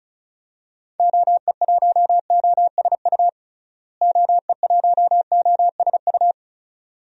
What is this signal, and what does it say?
Morse code, heard as a keyed tone, about 36 wpm, 700 Hz. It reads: OE1OSU OE1OSU
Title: DXCC-100 Call Signs - 6 Characters - Repeated 2x - 35wpm